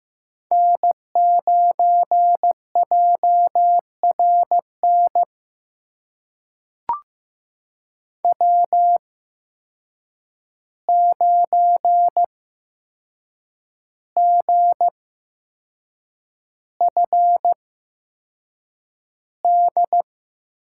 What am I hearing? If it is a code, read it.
Morse code, heard as a keyed tone, about 15 wpm, spaced as for 6 wpm, 700 Hz. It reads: N9JRN W9GFD
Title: USA Call Signs - 5 Characters - 8x Character Spacing - 15wpm